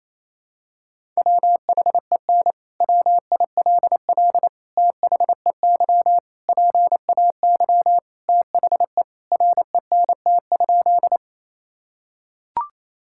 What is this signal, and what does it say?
Morse code, heard as a keyed tone, about 28 wpm, 700 Hz. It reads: WHEN WILL THEY PAY THE RENT?
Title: Sentences from Top 1000 Words - Review Part 5 - 28wpm